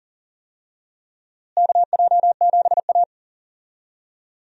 Morse code, heard as a keyed tone, about 40 wpm, 700 Hz. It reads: KJ7A
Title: USA Call Signs - 4 Characters - Repeated 3x - 40wpm